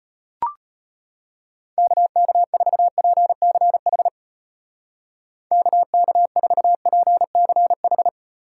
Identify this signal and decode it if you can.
Morse code, heard as a keyed tone, about 32 wpm, 700 Hz. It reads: KK4PCH KK4PCH